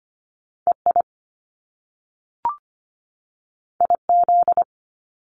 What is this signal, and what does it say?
Morse code, heard as a keyed tone, about 25 wpm, 700 Hz. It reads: EI IZ
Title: Top Two Letter Combinations in Words 25wpm